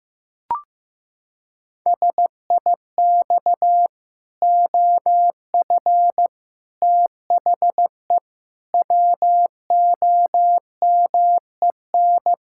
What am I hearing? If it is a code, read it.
Morse code, heard as a keyed tone, about 15 wpm, 700 Hz. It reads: SIX OF THE WOMEN
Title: Pangram Challenge v2 - 15wpm